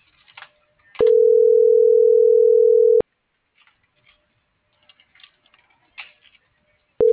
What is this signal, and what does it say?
Ringback tone (North America), 440 + 480 Hz.